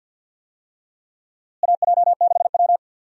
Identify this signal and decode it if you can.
Morse code, heard as a keyed tone, about 52 wpm, 700 Hz. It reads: AJ6W
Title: USA Call Signs - 4 Characters - Repeated 2x - 50wpm